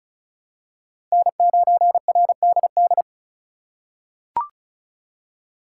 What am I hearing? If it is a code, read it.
Morse code, heard as a keyed tone, about 36 wpm, 700 Hz. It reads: N9RDD
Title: USA Call Signs - 5 Characters - 4x Character Spacing - 35wpm